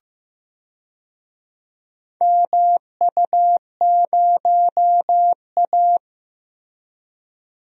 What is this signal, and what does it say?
Morse code, heard as a keyed tone, about 15 wpm, 700 Hz. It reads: MU0A
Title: DXCC-100 Call Signs - 4 Characters - Repeated 3x - 15wpm